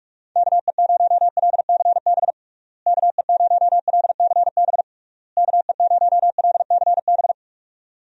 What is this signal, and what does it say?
Morse code, heard as a keyed tone, about 46 wpm, 700 Hz. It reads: KE0LCB KE0LCB KE0LCB
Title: USA Call Signs - 6 Characters - Repeated 3x - 45wpm